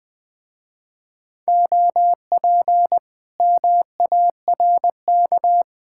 Morse code, heard as a keyed tone, about 20 wpm, 700 Hz. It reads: OP MARK
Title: Speed-Racing Common QSO Phrases - 17wpm